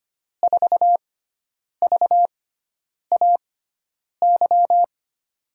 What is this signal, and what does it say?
Morse code, heard as a keyed tone, about 25 wpm, spaced as for 9 wpm, 700 Hz. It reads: 4VAY
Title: USA Call Signs - 6x Character Spacing - 25wpm